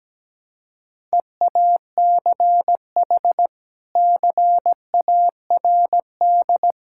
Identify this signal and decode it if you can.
Morse code, heard as a keyed tone, about 17 wpm, 700 Hz. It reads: EACH CARD
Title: Sentences from Top 1000 Words - Review Part 6 - 17wpm